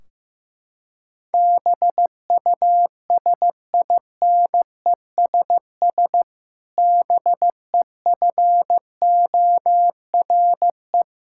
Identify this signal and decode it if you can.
Morse code, heard as a keyed tone, about 15 wpm, 700 Hz. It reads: BUSINESS BEFORE
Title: Sentences from Top 1600 Words 15wpm